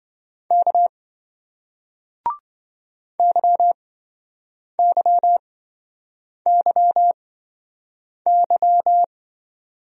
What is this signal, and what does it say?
Morse code, heard as a keyed tone, about 27 wpm, spaced as for 7 wpm, 700 Hz. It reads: K YYYY